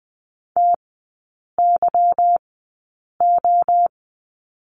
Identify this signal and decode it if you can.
Morse code, heard as a keyed tone, about 20 wpm, spaced as for 8 wpm, 690 Hz. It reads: TYO
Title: 145 Lesson 24e - Letter y - Three Characters 20wpm